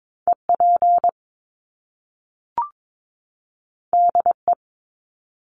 Morse code, heard as a keyed tone, about 22 wpm, 700 Hz. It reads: EP DE
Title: Top Two Letter Combinations in Words 22wpm